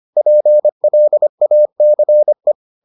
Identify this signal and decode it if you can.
Morse code, heard as a keyed tone, about 25 wpm, 600 Hz. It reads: PLACE